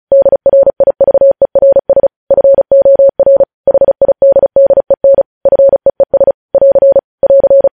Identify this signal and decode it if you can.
Morse code, heard as a keyed tone, about 36 wpm, 550 Hz. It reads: DRIVERS FOR HIDDEN FEES <AR> <AR>